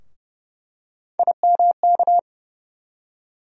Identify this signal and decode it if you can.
Morse code, heard as a keyed tone, about 30 wpm, 700 Hz. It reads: IMK
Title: USA Call Sign Postfixes - 3 Letters 30wpm